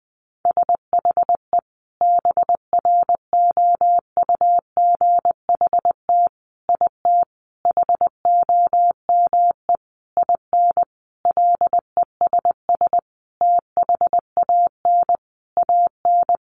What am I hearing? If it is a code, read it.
Morse code, heard as a keyed tone, about 20 wpm, 700 Hz. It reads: SHE BROUGHT IT HOME IN LESS THAN AN
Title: Sentences from Top 500 Words 20wpm